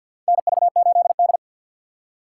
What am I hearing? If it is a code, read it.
Morse code, heard as a keyed tone, about 53 wpm, 700 Hz. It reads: NV8D
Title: USA Call Signs - 4 Characters - 6x Character Spacing - 50wpm